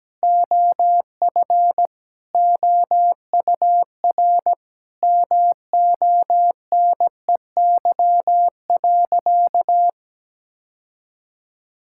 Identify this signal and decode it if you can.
Morse code, heard as a keyed tone, about 17 wpm, 700 Hz. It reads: OF OUR MONEY.
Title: Sentences from Top 1000 Words - Review Part 5 - 17wpm